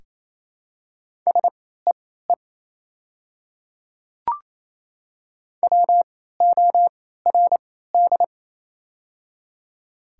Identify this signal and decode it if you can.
Morse code, heard as a keyed tone, about 28 wpm, spaced as for 16 wpm, 700 Hz. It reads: SEE WORD
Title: 3x Character Spacing - Top 200 Words - 28wpm